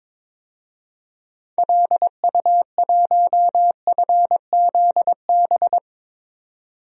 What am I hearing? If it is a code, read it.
Morse code, heard as a keyed tone, about 21 wpm, 700 Hz. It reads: LU1FZB
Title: DXCC-100 Call Signs - 6 Characters - 22wpm